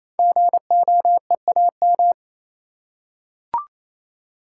Morse code, heard as a keyed tone, about 28 wpm, 700 Hz. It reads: GOEAM
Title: Code Groups - 5 Letters - 28wpm